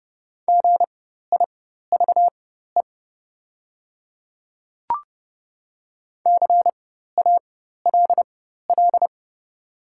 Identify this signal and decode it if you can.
Morse code, heard as a keyed tone, about 30 wpm, spaced as for 14 wpm, 690 Hz. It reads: GIVE CALL
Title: 4x Character Spacing - Top 200 Words - 30wpm